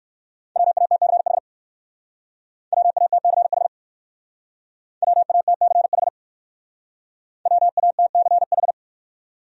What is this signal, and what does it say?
Morse code, heard as a keyed tone, about 56 wpm, 700 Hz. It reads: WATCH WATCH WATCH WATCH
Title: Speed-Racing Top 300 Words 45wpm